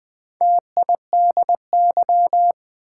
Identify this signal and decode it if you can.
Morse code, heard as a keyed tone, about 20 wpm, 700 Hz. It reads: TIDY
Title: Build A Word - 1 Letter At A Time - 4-Letter Words - 20wpm